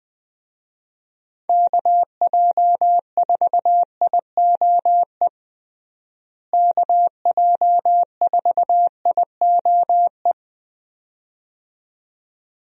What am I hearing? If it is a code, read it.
Morse code, heard as a keyed tone, about 20 wpm, 700 Hz. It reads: KJ4IOE KJ4IOE
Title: USA Call Signs - 6 Characters - Repeated 2x - 20wpm